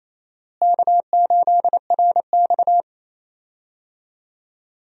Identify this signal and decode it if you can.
Morse code, heard as a keyed tone, about 27 wpm, 700 Hz. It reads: K8RX
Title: USA Call Signs - 4 Characters - Repeated 2x - 28wpm